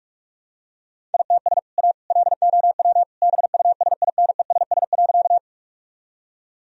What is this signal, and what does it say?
Morse code, heard as a keyed tone, about 46 wpm, 700 Hz. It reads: ITS A LOW BUSINESS.
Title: Sentences from Top 1000 Words - Review Part 1 - 45wpm